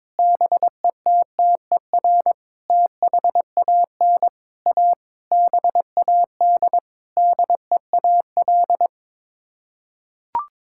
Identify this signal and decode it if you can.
Morse code, heard as a keyed tone, about 22 wpm, 700 Hz. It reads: BETTER THAN A BAD DEAL
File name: Sets of 9 Words - 22wpm